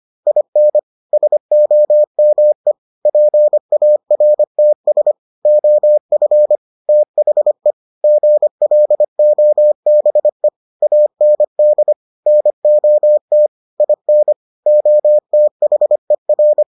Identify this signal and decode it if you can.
Morse code, heard as a keyed tone, about 25 wpm, 600 Hz. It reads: IN SOME PARTS OF THE GLOBE AND NOT IN OTHER